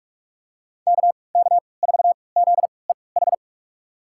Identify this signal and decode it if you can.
Morse code, heard as a keyed tone, about 44 wpm, spaced as for 25 wpm, 700 Hz. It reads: KK4ZEH
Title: USA Call Signs - 6 Characters - 3x Character Spacing - 45wpm